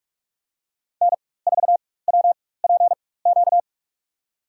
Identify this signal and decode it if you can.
Morse code, heard as a keyed tone, about 45 wpm, spaced as for 21 wpm, 700 Hz. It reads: N4WPQ
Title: USA Call Signs - 4x Character Spacing - 45wpm